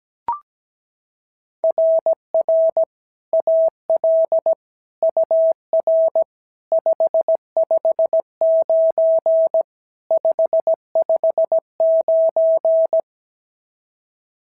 Morse code, heard as a keyed tone, about 16 wpm, 650 Hz. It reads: RR AL UR 559 559
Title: SOTA QSO Practice 17wpm